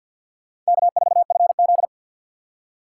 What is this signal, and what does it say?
Morse code, heard as a keyed tone, about 61 wpm, 700 Hz. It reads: K4FZ
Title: Speed Racing - USA 1x2 Call Signs 50wpm